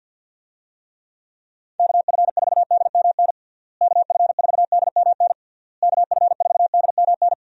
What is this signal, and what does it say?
Morse code, heard as a keyed tone, about 51 wpm, 700 Hz. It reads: KF4DMN KF4DMN KF4DMN
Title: USA Call Signs - 6 Characters - Repeated 3x - 50wpm